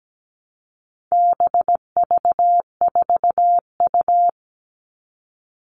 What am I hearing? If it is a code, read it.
Morse code, heard as a keyed tone, about 17 wpm, 700 Hz. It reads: BV4U
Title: All Cut Numbers - 4-Digit 17wpm